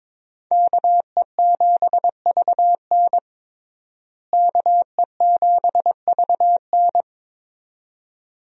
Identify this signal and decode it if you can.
Morse code, heard as a keyed tone, about 22 wpm, 700 Hz. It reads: KE7VN KE7VN
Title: USA Call Signs - 5 Characters - Repeated 2x - 22wpm